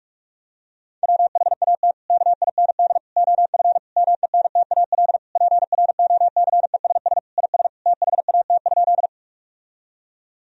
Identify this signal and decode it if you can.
Morse code, heard as a keyed tone, about 46 wpm, 700 Hz. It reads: WHAT KIND OF MENTAL PROCESS IS THAT?
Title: Sentences from Top 1000 Words - Review Part 4 - 45wpm